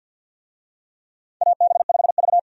Morse code, heard as a keyed tone, about 51 wpm, 700 Hz. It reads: AB5V
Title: USA Call Signs - 4 Characters - 50wpm